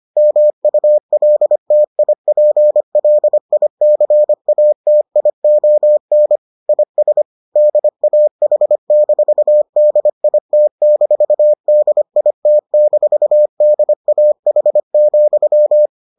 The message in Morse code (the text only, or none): MULTIPLICATION IS DAH-DIT-DIT-DAH,